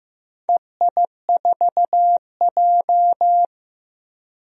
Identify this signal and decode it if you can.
Morse code, heard as a keyed tone, about 16 wpm, 700 Hz. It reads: EI4J
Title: DXCC-100 Call Signs - 4 Characters - Repeated 3x - 15wpm